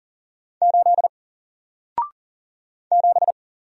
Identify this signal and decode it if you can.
Morse code, heard as a keyed tone, about 40 wpm, 700 Hz. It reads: 8 7
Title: Single Letter-Number Alt 40wpm